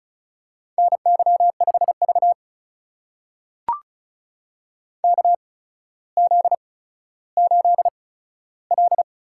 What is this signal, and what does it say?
Morse code, heard as a keyed tone, about 36 wpm, spaced as for 15 wpm, 700 Hz. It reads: NY5V KZ8L
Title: USA Call Signs - 4 Characters - 8x Character Spacing - 35wpm